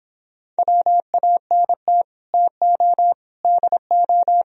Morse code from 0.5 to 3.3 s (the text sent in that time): WANT TO